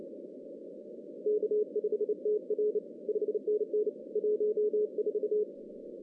Morse code, heard as a keyed tone, about 31 wpm, 420 Hz. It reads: K5TR 5NN 14
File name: 2015 cq ww cw / 2015 cqwww k5tr - 0-41